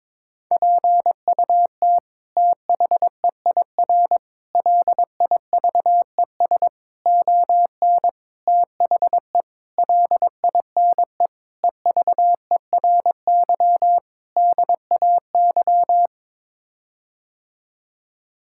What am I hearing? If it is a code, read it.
Morse code, heard as a keyed tone, about 22 wpm, 700 Hz. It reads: PUT THEIR LIVES ON THE LINE EVERY DAY
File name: practice / 22wpm / Sets of 8 Words - 22wpm